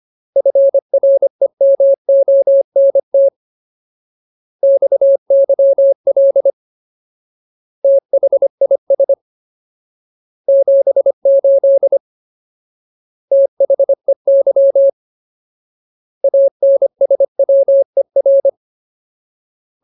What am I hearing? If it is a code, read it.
Morse code, heard as a keyed tone, about 25 wpm, 550 Hz. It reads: FREMONT XYL THIS 78 THEY ANSWER
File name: Ultimate Word List / 25WPM Ultimate